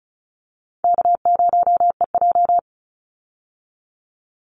Morse code, heard as a keyed tone, about 34 wpm, 700 Hz. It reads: K0EJ
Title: Call Signs Repeated 2x - 35wpm